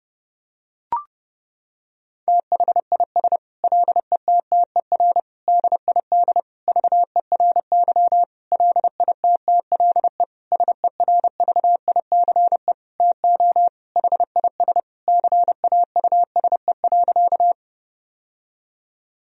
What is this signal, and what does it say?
Morse code, heard as a keyed tone, about 30 wpm, 700 Hz. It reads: THIS LETTER DID VERY LITTLE SERVICE TO HIS CAUSE.
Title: Sentences from Top 1000 Words - Review Part 4 - 30wpm